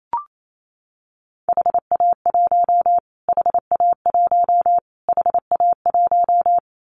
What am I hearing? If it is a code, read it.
Morse code, heard as a keyed tone, about 28 wpm, 700 Hz. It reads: HA1 HA1 HA1
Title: Callsign Prefixes for Top 25 DXCC - Repeated 3x - 28wpm